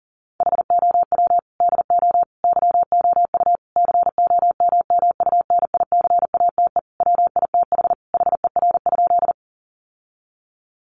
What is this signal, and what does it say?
Morse code, heard as a keyed tone, about 40 wpm, 700 Hz. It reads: HOW DO YOU COMMUNICATE WITH HER?
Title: Sentences from Top 1300 Words 40wpm